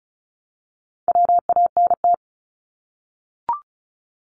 Morse code, heard as a keyed tone, about 34 wpm, 700 Hz. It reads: WANT
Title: Top 200 Words 35wpm